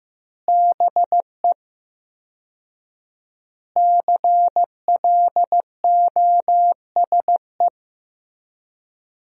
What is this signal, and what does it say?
Morse code, heard as a keyed tone, about 15 wpm, 700 Hz. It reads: BE CLOSE
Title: 4x Word-Spacing - Sentences from Top 300 Words - 15wpm